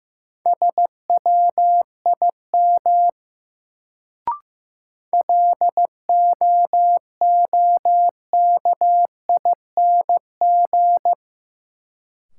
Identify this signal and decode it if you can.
Morse code, heard as a keyed tone, about 15 wpm, 700 Hz. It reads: SWIM LOOKING
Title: Sentences from Top 1000 Words - Review Part 6 - New Word Drill - 15wpm